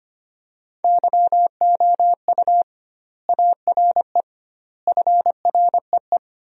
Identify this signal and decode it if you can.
Morse code, heard as a keyed tone, about 25 wpm, 700 Hz. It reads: YOU ARE FREE
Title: 2x Word-Spacing - Sentences from Top 500 Words - 25wpm